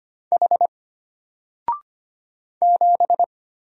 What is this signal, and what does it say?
Morse code, heard as a keyed tone, about 25 wpm, 700 Hz. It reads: H 7
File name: Single-Letter-Number-25wpm